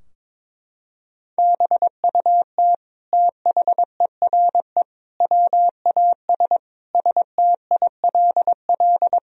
Morse code, heard as a keyed tone, about 22 wpm, 700 Hz. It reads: BUT THERE WAS STILL
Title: Sentences from Top 1000 Words - Review Part 2 - 22wpm